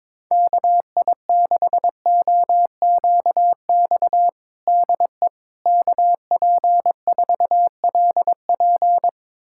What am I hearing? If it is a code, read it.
Morse code, heard as a keyed tone, about 22 wpm, 700 Hz. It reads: KI6OQX DE KP4LP
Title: USA Callsign DE Callsign 22wpm